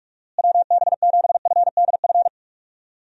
Morse code, heard as a keyed tone, about 46 wpm, 700 Hz. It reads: WB7FDF
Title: USA Call Signs - 6 Characters - Repeated 2x - 45wpm